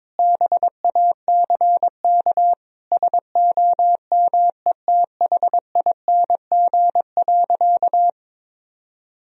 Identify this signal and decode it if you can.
Morse code, heard as a keyed tone, about 22 wpm, 700 Hz. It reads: BACK SOMETHING.
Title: Sentences from Top 1000 Words - Review Part 3 - 22wpm